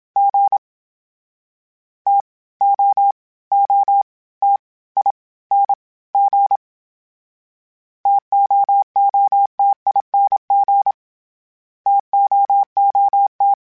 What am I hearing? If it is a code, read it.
Morse code, heard as a keyed tone, about 27 wpm, 800 Hz. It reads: G T O O T I N G TOOTING TOOT